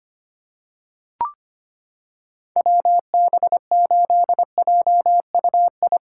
Morse code, heard as a keyed tone, about 25 wpm, 700 Hz. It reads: WB8JUI
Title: K1USN SST QSO Practice - 25wpm